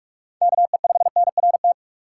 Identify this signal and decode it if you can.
Morse code, heard as a keyed tone, about 46 wpm, 700 Hz. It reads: KE5NRT
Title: USA Call Signs - 2x Character Spacing - 45wpm